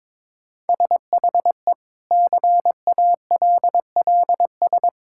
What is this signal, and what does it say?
Morse code, heard as a keyed tone, about 22 wpm, 700 Hz. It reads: SHE CALLS
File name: Difficult Sentences with Hints - 22wpm